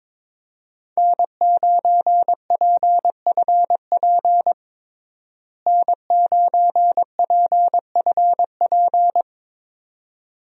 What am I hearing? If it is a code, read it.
Morse code, heard as a keyed tone, about 22 wpm, 700 Hz. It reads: N9PFP N9PFP
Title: USA Call Signs - 5 Characters - Repeated 2x - 22wpm